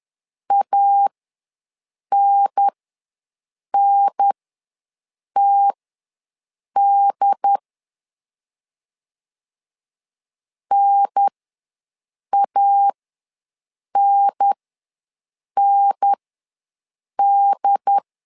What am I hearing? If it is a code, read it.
Morse code, heard as a keyed tone, about 10 wpm, spaced as for 6 wpm, 800 Hz. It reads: ANNTD NANND
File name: 2 UHDN ws 10wpm